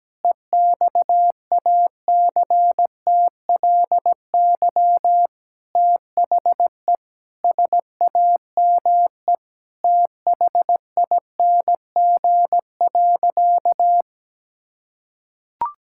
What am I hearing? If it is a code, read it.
Morse code, heard as a keyed tone, about 17 wpm, 700 Hz. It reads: EXACTLY THE SAME THING.
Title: Sentences from Top 1000 Words - Review Part 6 - 17wpm